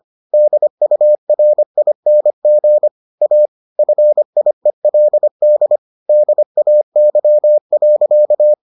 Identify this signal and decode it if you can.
Morse code, heard as a keyed tone, about 25 wpm, 600 Hz. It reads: DURING A FIELD DAY.